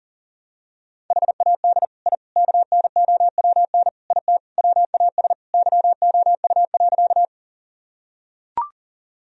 Morse code, heard as a keyed tone, about 40 wpm, 700 Hz. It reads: HAD I KNOWN IT WAS YOU.
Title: Sentences from Top 1000 Words - Review Part 3 - 40wpm